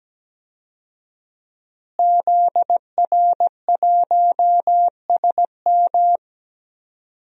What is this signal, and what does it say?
Morse code, heard as a keyed tone, about 17 wpm, 700 Hz. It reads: ZR1SM
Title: DXCC-100 Call Signs - 5 Characters - Repeated 3x - 17wpm